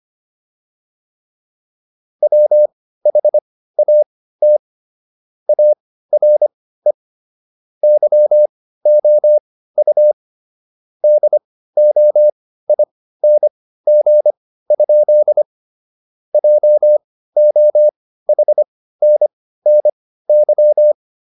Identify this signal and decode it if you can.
Morse code, heard as a keyed tone, about 25 wpm, spaced as for 15 wpm, 600 Hz. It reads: WHAT ARE YOU DOING? JOHNNY